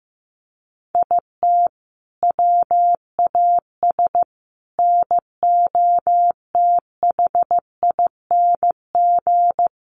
Morse code, heard as a keyed tone, about 15 wpm, 700 Hz. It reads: IT WAS NOTHING
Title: Sentences from Top 1300 Words 15wpm